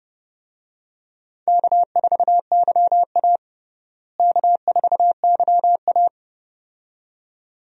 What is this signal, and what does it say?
Morse code, heard as a keyed tone, about 31 wpm, 700 Hz. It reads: K4YA K4YA
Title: USA Call Signs - 4 Characters - Repeated 2x - 30wpm